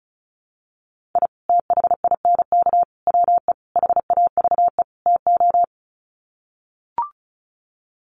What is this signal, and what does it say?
Morse code, heard as a keyed tone, about 34 wpm, 700 Hz. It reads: I THINK WE HAVE TO
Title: Sets of 5 Words 35wpm